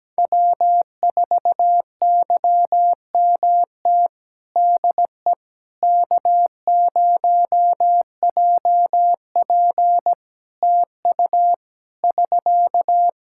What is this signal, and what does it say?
Morse code, heard as a keyed tone, about 17 wpm, 700 Hz. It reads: W4YMT DE K0JP TU <SK>
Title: Practice QSO - Exchange 4 - Ending - 17wpm